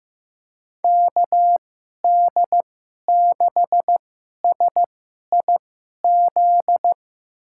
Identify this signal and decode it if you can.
Morse code, heard as a keyed tone, about 15 wpm, spaced as for 11 wpm, 700 Hz. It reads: KD6SIZ
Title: USA Call Signs - 6 Characters - 2x Character Spacing - 15wpm